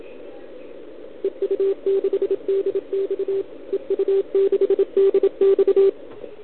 Morse code, heard as a keyed tone, about 27 wpm, 410 Hz. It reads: EU6DX EU6DX
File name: ARCHIV / OKDX2020 / OL4Wa